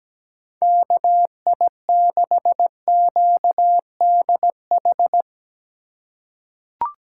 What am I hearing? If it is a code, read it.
Morse code, heard as a keyed tone, about 17 wpm, 700 Hz. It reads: KI6QDH
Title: K1USN SST QSO Practice - 17wpm